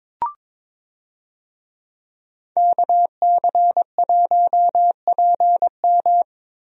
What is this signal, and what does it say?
Morse code, heard as a keyed tone, about 22 wpm, 700 Hz. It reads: KC1PM
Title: USA Call Signs - 5 Characters - Repeated 2x - 22wpm